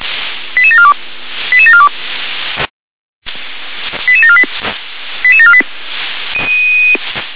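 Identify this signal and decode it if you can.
Five-tone selective call (ZVEI), heard as quick runs of five tones, tones of about 70 ms. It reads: 88742 88742 88748 88748
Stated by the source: ZVEI five-tone call 88509 received by SDR.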